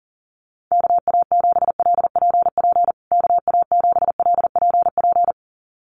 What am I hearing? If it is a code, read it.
Morse code, heard as a keyed tone, about 40 wpm, 700 Hz. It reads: KA7LPP KA7LPP
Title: USA Call Signs Repeated 2x 40wpm